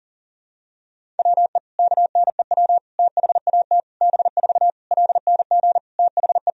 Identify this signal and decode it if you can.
Morse code, heard as a keyed tone, about 40 wpm, 700 Hz. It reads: WE KNEW THAT B4 LNG THE
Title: Sentences from Top 400 Words and Abbreviations 40wpm